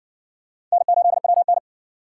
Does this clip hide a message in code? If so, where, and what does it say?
Yes: Morse code from 0.7 to 1.6 s, heard as a keyed tone, about 60 wpm, 700 Hz. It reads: N8WN